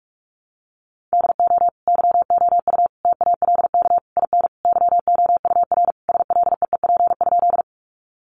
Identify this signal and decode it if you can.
Morse code, heard as a keyed tone, about 44 wpm, 700 Hz. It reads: DO YOU TALK IN YOUR SLEEP?
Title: Sentences from Top 500 Words 45wpm